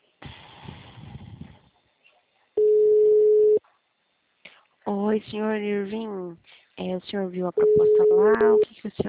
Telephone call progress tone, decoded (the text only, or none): ringback tone